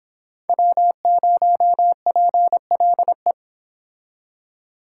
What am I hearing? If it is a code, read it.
Morse code, heard as a keyed tone, about 30 wpm, 700 Hz. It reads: W0PLE